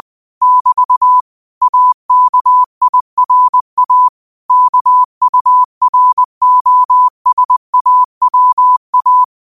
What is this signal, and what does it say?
Morse code, heard as a keyed tone, about 20 wpm, 1000 Hz. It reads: <BT> AKIRA KUROSAWA
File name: qotdfiles / 20wpm